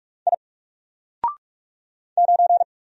Morse code, heard as a keyed tone, about 46 wpm, 700 Hz. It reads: I 9
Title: Single Letter-Number 45wpm